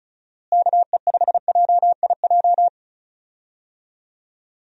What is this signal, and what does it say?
Morse code, heard as a keyed tone, about 36 wpm, 700 Hz. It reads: KE5JIJ
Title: USA Call Signs - 6 Characters - Repeated 3x - 35wpm